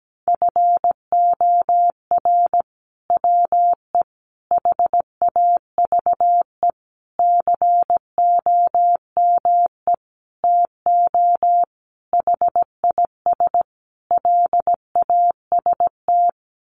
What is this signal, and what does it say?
Morse code, heard as a keyed tone, about 17 wpm, 700 Hz. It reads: FOR WE HAVE COME TO HIS LAST
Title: Sentences from Top 1000 Words 17wpm